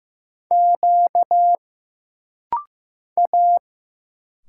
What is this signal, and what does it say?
Morse code, heard as a keyed tone, about 15 wpm, 700 Hz. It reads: Q A
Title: Single Letters - Rapid-Fire 15wpm